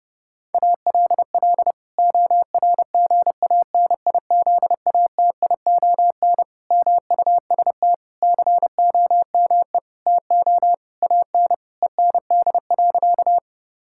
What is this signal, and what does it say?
Morse code, heard as a keyed tone, about 30 wpm, 700 Hz. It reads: ALL ORGANIZATION MUST COME TO AN END.
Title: Sentences from Top 1000 Words - Review Part 5 - 30wpm